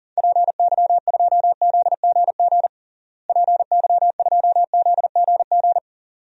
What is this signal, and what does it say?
Morse code, heard as a keyed tone, about 40 wpm, 700 Hz. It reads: PY2ZGG PY2ZGG
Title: DXCC-100 Call Signs - 6 Characters - Repeated 2x - 40wpm